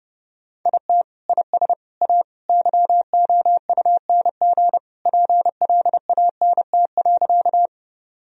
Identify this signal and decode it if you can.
Morse code, heard as a keyed tone, about 30 wpm, 700 Hz. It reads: IT IS A YOUNG PLANT.
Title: Sentences from Top 1000 Words - Review Part 1 - 30wpm